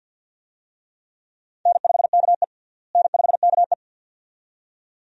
Morse code, heard as a keyed tone, about 46 wpm, 700 Hz. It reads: N5KE N5KE
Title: USA Call Signs - 4 Characters - Repeated 2x - 50wpm